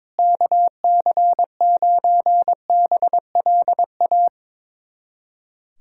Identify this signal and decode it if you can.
Morse code, heard as a keyed tone, about 21 wpm, 700 Hz. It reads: KC9BLA
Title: USA Callsign DE Callsign 22wpm